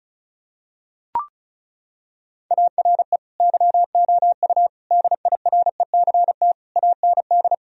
Morse code, heard as a keyed tone, about 36 wpm, 700 Hz. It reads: ARE YOU DIRECT AND
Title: Sentences from Top 1000 Words - Review Part 4 - 35wpm